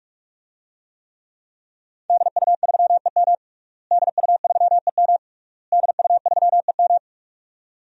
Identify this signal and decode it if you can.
Morse code, heard as a keyed tone, about 46 wpm, 700 Hz. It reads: DU3EM DU3EM DU3EM
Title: DXCC-100 Call Signs - 5 Characters - Repeated 3x - 45wpm